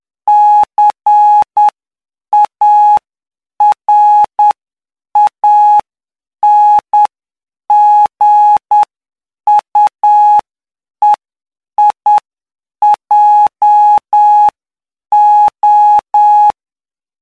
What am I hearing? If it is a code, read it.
Morse code, heard as a keyed tone, about 9 wpm, spaced as for 7 wpm, 840 Hz. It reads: CARANGUEIJO
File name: cw / py2lcd / 07-PY2LCD-Li��o 16